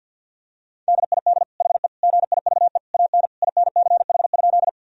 Morse code, heard as a keyed tone, about 52 wpm, 700 Hz. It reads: DID HE GIVE AN INCH?
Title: Sentences from Top 1000 Words - Review Part 3 - 50wpm